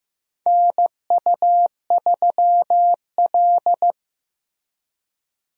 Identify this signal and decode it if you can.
Morse code, heard as a keyed tone, about 15 wpm, 700 Hz. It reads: NU3L
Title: USA Call Signs - 4 Characters - Repeated 3x - 15wpm